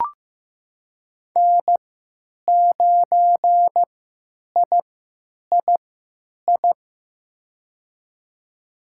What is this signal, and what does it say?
Morse code, heard as a keyed tone, about 15 wpm, spaced as for 9 wpm, 700 Hz. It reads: N9III
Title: USA Call Signs - 3x Character Spacing - 15wpm